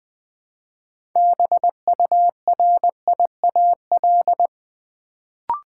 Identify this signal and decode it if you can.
Morse code, heard as a keyed tone, about 20 wpm, 700 Hz. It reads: BURIAL
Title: Build A Word - 3 Letters At A Time - 6-Letter Words - 20wpm